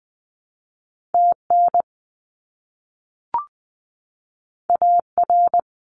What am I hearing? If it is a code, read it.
Morse code, heard as a keyed tone, about 20 wpm, 700 Hz. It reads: TN AR
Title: US States Repeated 3x 20wpm